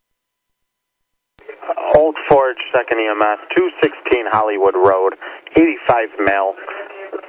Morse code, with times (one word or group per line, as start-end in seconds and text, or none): none